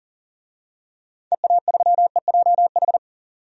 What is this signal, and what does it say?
Morse code, heard as a keyed tone, about 40 wpm, 700 Hz. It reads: EA3EJH
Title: DXCC-100 Call Signs - 6 Characters - 40wpm